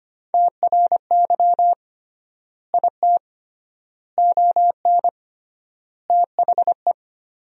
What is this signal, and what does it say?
Morse code, heard as a keyed tone, about 25 wpm, 700 Hz. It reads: TRY IT ON THE
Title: 3x Word-Spacing - Sentences from Top 400 Words - 25wpm